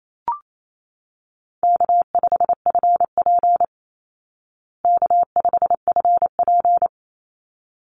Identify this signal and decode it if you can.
Morse code, heard as a keyed tone, about 28 wpm, 700 Hz. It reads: K5FP K5FP
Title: Easy Call Signs Repeated 2x - 28wpm